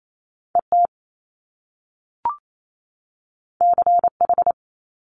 Morse code, heard as a keyed tone, about 28 wpm, 700 Hz. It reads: ET CH
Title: Top Two Letter Combinations in Words 28wpm